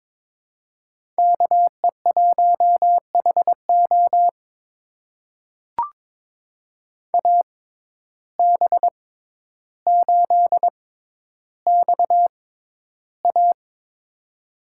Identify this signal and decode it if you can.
Morse code, heard as a keyed tone, about 22 wpm, spaced as for 11 wpm, 700 Hz. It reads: KE1HO AB8XA
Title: USA Call Signs - 5 Characters - 6x Character Spacing - 22wpm